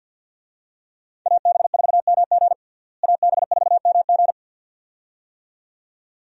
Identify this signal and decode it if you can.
Morse code, heard as a keyed tone, about 53 wpm, 700 Hz. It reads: AB4MG AB4MG
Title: USA Call Signs - 5 Characters - Repeated 2x - 50wpm